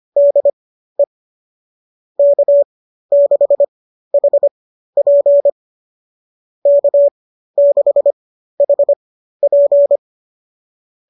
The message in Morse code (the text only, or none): DE K6HP K6HP